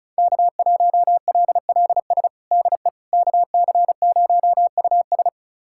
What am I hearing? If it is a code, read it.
Morse code, heard as a keyed tone, about 36 wpm, 700 Hz. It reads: K1LLS DE KC0US
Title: USA Callsign DE Callsign - 5 Characters - 35wpm